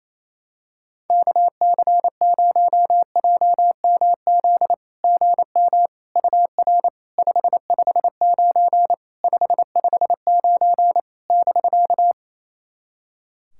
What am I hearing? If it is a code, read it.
Morse code, heard as a keyed tone, about 28 wpm, 700 Hz. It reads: KC0JMZ GM UR 559 559 <BK>
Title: SOTA QSO Practice 28wpm